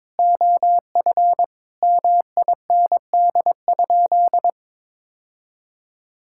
Morse code, heard as a keyed tone, about 22 wpm, 700 Hz. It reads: OF MIND?
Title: Sentences from Top 1000 Words - Review Part 6 - 22wpm